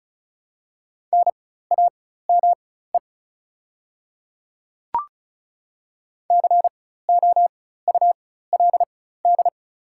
Morse code, heard as a keyed tone, about 36 wpm, spaced as for 16 wpm, 700 Hz. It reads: NAME COULD
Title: 4x Character Spacing - Top 200 Words - 35wpm